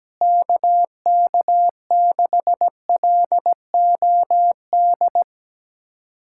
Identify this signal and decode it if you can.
Morse code, heard as a keyed tone, about 17 wpm, 700 Hz. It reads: KK6LOD
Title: USA Callsign DE Callsign - 6 Characters - 17wpm